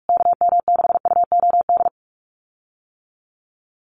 Morse code, heard as a keyed tone, about 44 wpm, 700 Hz. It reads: KM6UOD
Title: USA Call Signs Repeated 2x 45wpm